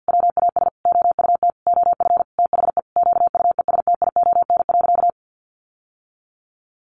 Morse code, heard as a keyed tone, about 50 wpm, 700 Hz. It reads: WAS OUT OF THE QUESTION.
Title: Sentences from Top 600 Words 50wpm